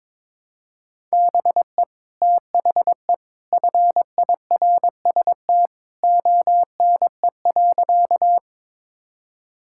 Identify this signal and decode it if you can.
Morse code, heard as a keyed tone, about 22 wpm, 700 Hz. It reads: BE THE FIRST ONE.
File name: practice / 22wpm / Sentences with 2-Word Phrases - 22wpm